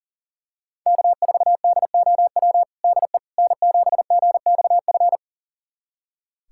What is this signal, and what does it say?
Morse code, heard as a keyed tone, about 40 wpm, 700 Hz. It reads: K4DOW DE N7GXF
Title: USA Callsign DE Callsign - 5 Characters - 40wpm